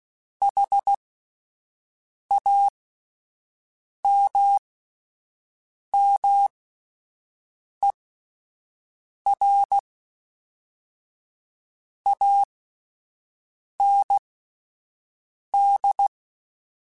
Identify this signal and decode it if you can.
Morse code, heard as a keyed tone, about 16 wpm, spaced as for 5 wpm, 800 Hz. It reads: HAMMER AND
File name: Misc / Misc-mp3s / Word Pairs @5